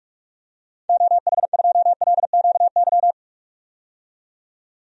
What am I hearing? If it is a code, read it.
Morse code, heard as a keyed tone, about 46 wpm, 700 Hz. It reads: OH2LQY
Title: DXCC-100 Call Signs - 6 Characters - 45wpm